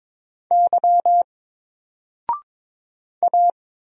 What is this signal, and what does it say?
Morse code, heard as a keyed tone, about 22 wpm, 700 Hz. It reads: Y A
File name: Single-Letter-Number-22wpm